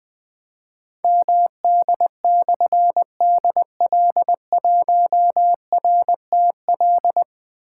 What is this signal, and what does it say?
Morse code, heard as a keyed tone, about 20 wpm, 700 Hz. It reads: MD/DL1RTL
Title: Difficult Call Signs 20wpm